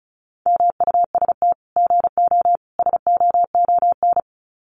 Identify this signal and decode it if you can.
Morse code, heard as a keyed tone, about 34 wpm, 700 Hz. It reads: MUST GO SOON
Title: Common QSO Phrases 35wpm